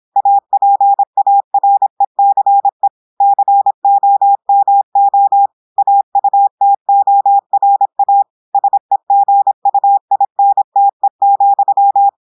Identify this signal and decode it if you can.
Morse code, heard as a keyed tone, about 26 wpm, 800 Hz. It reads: APARECE COMO AUTORA SEGUINTE,
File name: audios / portugais / portugais texte en clair 1 vitesse 26 mots minutes